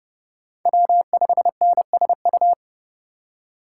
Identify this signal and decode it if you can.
Morse code, heard as a keyed tone, about 30 wpm, 700 Hz. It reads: W5NSU